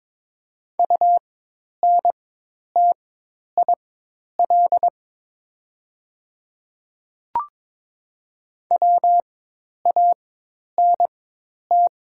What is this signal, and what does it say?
Morse code, heard as a keyed tone, about 23 wpm, spaced as for 10 wpm, 700 Hz. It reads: UNTIL WANT
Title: 4x Character Spacing - Top 300 Words - 22wpm